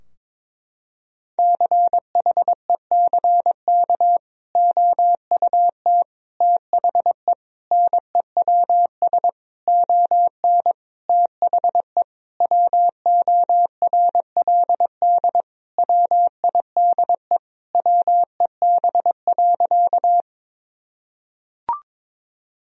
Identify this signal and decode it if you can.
Morse code, heard as a keyed tone, about 21 wpm, 700 Hz. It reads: CHECK OUT THE NEWS ON THE WORLD WIDE WEB.
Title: Sentences from Top 1000 Words - Review Part 4 - 22wpm